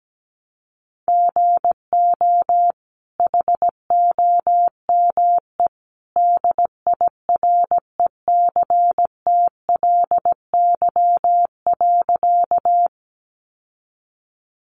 Morse code, heard as a keyed tone, about 17 wpm, 700 Hz. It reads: GO HOME DIRECTLY.
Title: Sentences from Top 800 Words 17wpm